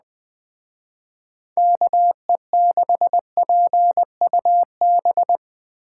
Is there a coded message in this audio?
Yes: Morse code from 1.6 to 5.4 s, heard as a keyed tone, about 21 wpm, 700 Hz. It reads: KE6PUB